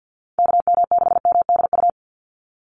Morse code, heard as a keyed tone, about 49 wpm, 700 Hz. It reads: KM6MDU
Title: USA Call Signs 50wpm